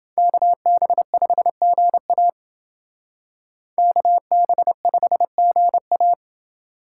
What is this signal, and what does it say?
Morse code, heard as a keyed tone, about 30 wpm, 700 Hz. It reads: KB5GA KB5GA